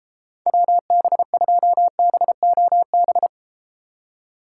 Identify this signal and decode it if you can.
Morse code, heard as a keyed tone, about 33 wpm, 700 Hz. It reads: WB2BOB